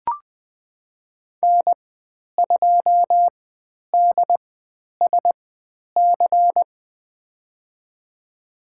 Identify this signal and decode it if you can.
Morse code, heard as a keyed tone, about 20 wpm, spaced as for 10 wpm, 700 Hz. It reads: N2DSC
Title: 109 Lesson 18g - Letter m - Call Signs 20wpm